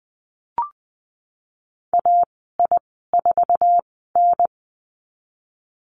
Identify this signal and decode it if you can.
Morse code, heard as a keyed tone, about 20 wpm, spaced as for 15 wpm, 700 Hz. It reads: AI4N
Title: 062 Lesson 11g - Letter h - Call Signs 20wpm